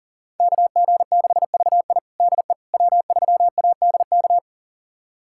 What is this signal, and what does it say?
Morse code, heard as a keyed tone, about 40 wpm, 700 Hz. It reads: KG6VI DE W3ADK
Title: USA Callsign DE Callsign - 5 Characters - 40wpm